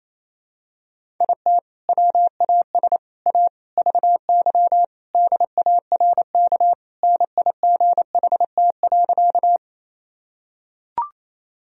Morse code, heard as a keyed tone, about 28 wpm, 700 Hz. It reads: IT WAS A VY DARK NIGHT.
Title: Sentences from Top 500 Words and Abbreviations 28wpm